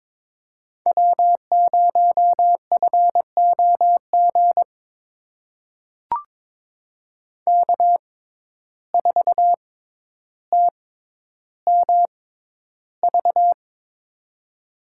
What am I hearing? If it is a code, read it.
Morse code, heard as a keyed tone, about 22 wpm, spaced as for 11 wpm, 700 Hz. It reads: W0FOG K4TMV
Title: USA Call Signs - 5 Characters - 6x Character Spacing - 22wpm